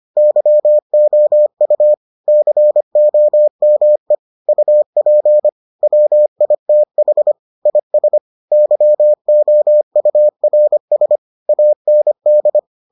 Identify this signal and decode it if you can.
Morse code, heard as a keyed tone, about 25 wpm, 600 Hz. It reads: YOU COME UP WITH IS YOURS AND